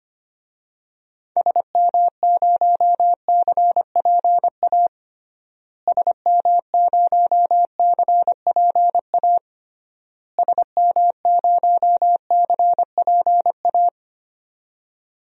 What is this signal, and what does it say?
Morse code, heard as a keyed tone, about 25 wpm, 700 Hz. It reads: SM0CPA SM0CPA SM0CPA
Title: DXCC-100 Call Signs - 6 Characters - Repeated 3x - 25wpm